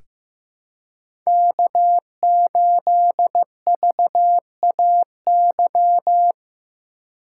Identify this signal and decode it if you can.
Morse code, heard as a keyed tone, about 15 wpm, 700 Hz. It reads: K8VAY
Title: USA Call Signs - 5 Characters - 4x Character Spacing - 15wpm